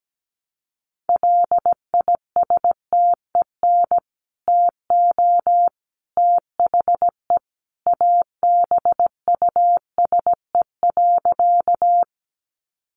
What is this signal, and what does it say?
Morse code, heard as a keyed tone, about 17 wpm, 700 Hz. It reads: LISTEN TO THE ABUSE.
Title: Sentences from Top 1300 Words 17wpm